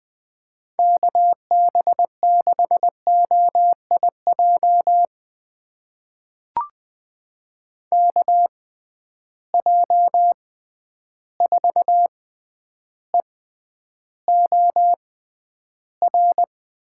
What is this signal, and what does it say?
Morse code, heard as a keyed tone, about 20 wpm, spaced as for 10 wpm, 700 Hz. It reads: KB6OIJ KJ4EOR
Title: USA Call Signs - 6 Characters - 6x Character Spacing - 20wpm